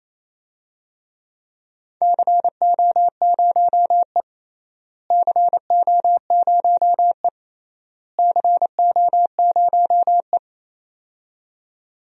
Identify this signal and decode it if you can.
Morse code, heard as a keyed tone, about 28 wpm, 700 Hz. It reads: CO0E CO0E CO0E
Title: DXCC-100 Call Signs - 4 Characters - Repeated 3x - 28wpm